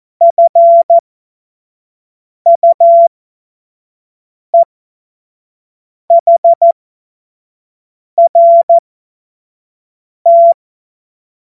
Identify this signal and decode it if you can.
Morse code, heard as a keyed tone, about 14 wpm, spaced as for 5 wpm, 680 Hz. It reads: FUEHRT
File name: Morsekurs / Witze / Witz 27 Februar 2024 25 F70